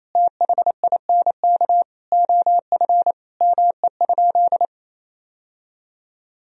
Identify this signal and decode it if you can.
Morse code, heard as a keyed tone, about 28 wpm, 700 Hz. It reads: THINK OF ME?
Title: Sentences from Top 1000 Words - Review Part 3 - 28wpm